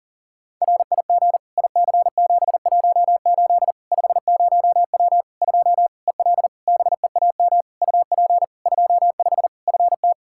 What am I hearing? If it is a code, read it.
Morse code, heard as a keyed tone, about 40 wpm, 700 Hz. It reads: RIG IC718 50W 2 EL BEAM UP 25 FT
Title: K1USN SST QSO Practice 40wpm